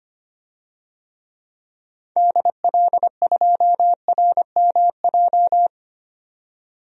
Morse code, heard as a keyed tone, about 25 wpm, 700 Hz. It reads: DL2RMJ
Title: DXCC-100 Call Signs - 6 Characters - 25wpm